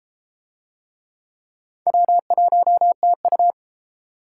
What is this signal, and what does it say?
Morse code, heard as a keyed tone, about 31 wpm, 700 Hz. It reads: W1TU